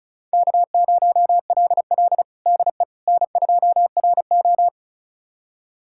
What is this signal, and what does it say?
Morse code, heard as a keyed tone, about 34 wpm, 700 Hz. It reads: K0LL DE N2RO
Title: USA Callsign DE Callsign - 4 Characters - 35wpm